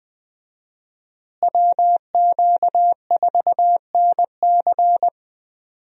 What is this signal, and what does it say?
Morse code, heard as a keyed tone, about 20 wpm, 700 Hz. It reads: WQ4NC
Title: USA Call Signs - 5 Characters - Repeated 3x - 20wpm